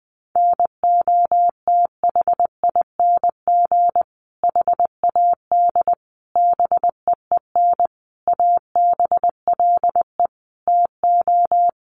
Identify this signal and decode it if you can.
Morse code, heard as a keyed tone, about 20 wpm, 700 Hz. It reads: NOTHING HAD BEEN ABLE TO
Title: Sentences from Top 1300 Words 20wpm